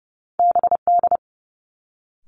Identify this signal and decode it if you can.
Morse code, heard as a keyed tone, about 30 wpm, 700 Hz. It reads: BD
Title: Two Challenging Letter-Number Combinations 30wpm